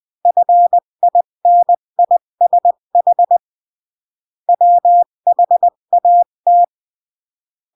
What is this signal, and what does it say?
Morse code, heard as a keyed tone, about 20 wpm, 700 Hz. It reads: FINISH WHAT
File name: CW / jodorowsky / 82 commandments 20-15